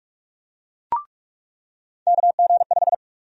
Morse code, heard as a keyed tone, about 46 wpm, 700 Hz. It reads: KG5
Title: USA Call Sign Prefixes - 45wpm